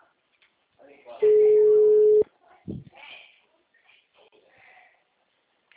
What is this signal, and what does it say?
Ringback tone (Europe, 425 Hz), with speech.